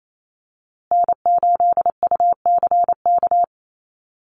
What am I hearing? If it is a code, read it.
Morse code, heard as a keyed tone, about 28 wpm, 700 Hz. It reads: N8UCK
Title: USA Call Signs Repeated 2x 28wpm